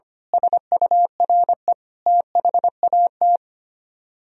Morse code, heard as a keyed tone, about 25 wpm, 700 Hz. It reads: SURE THAT
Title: Speed-Racing - Sets of 5 Words 25wpm